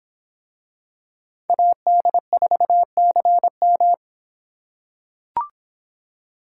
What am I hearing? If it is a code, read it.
Morse code, heard as a keyed tone, about 25 wpm, 700 Hz. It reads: AD4CM